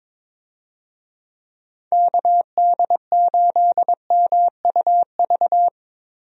Morse code, heard as a keyed tone, about 22 wpm, 700 Hz. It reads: KD8MUV
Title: USA Call Signs - 6 Characters - Repeated 3x - 22wpm